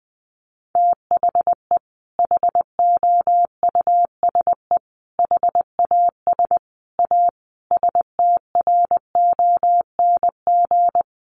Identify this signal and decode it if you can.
Morse code, heard as a keyed tone, about 20 wpm, 700 Hz. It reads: THE HOUSE HAS A STRONG
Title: Sentences from Top 1200 Words 20wpm